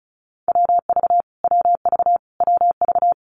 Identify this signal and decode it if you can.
Morse code, heard as a keyed tone, about 34 wpm, 700 Hz. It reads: WV WV WV
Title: US States Repeated 3x 35wpm